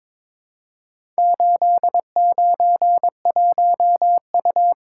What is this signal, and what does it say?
Morse code, heard as a keyed tone, about 22 wpm, 700 Hz. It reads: 891U
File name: practice / 22wpm / Four Letter-Number Combinations - 22wpm